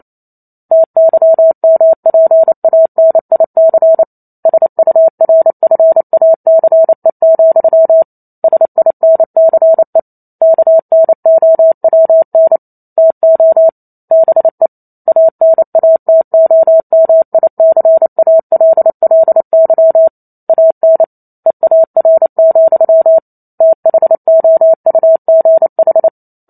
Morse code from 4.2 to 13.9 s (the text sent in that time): SURFACE, SINCE KNOWN TO